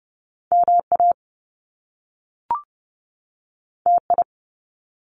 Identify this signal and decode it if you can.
Morse code, heard as a keyed tone, about 30 wpm, 700 Hz. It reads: MA TI
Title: Top Two Letter Combinations in Words 30wpm